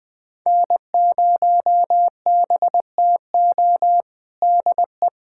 Morse code, heard as a keyed tone, about 20 wpm, 700 Hz. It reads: N0BTO DE
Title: USA Callsign DE Callsign - 5 Characters - 20wpm